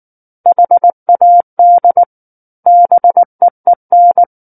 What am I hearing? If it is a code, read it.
Morse code, heard as a keyed tone, about 19 wpm, 700 Hz. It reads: HAD BEEN
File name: Book One / 10-Chapter 10 - 19 WPM